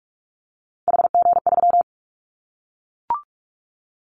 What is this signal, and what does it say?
Morse code, heard as a keyed tone, about 46 wpm, 700 Hz. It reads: HG3
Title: Callsign Prefixes for Top 25 DXCC - 45wpm